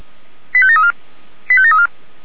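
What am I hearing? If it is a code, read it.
Five-tone selective call (ZVEI), heard as quick runs of five tones, tones of about 70 ms. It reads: 85734 85734
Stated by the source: ZVEI five-tone selective call received on SDR, Feuerwehr Orlamünde audio archive.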